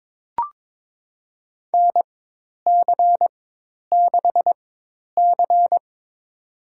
Morse code, heard as a keyed tone, about 22 wpm, spaced as for 10 wpm, 700 Hz. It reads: NC6C
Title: USA Call Signs - 4 Characters - 4x Character Spacing - 22wpm